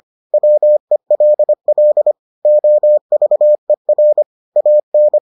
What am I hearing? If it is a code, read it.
Morse code, heard as a keyed tone, about 25 wpm, 600 Hz. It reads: WELL OVER AN